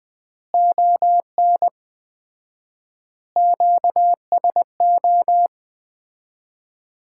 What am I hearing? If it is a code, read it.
Morse code, heard as a keyed tone, about 21 wpm, 700 Hz. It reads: ON QSO
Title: Common QSO Phrases - 4x Word-Spacing - 20wpm